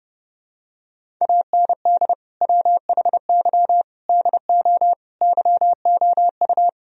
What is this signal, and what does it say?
Morse code, heard as a keyed tone, about 32 wpm, 700 Hz. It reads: AND WHY DO YOU
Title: Sentences from Top 1000 Words - Review Part 2 - 30wpm